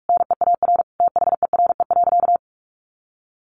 Morse code, heard as a keyed tone, about 44 wpm, 700 Hz. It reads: NEAR THERE.
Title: Sentences from Top 1400 Words 45wpm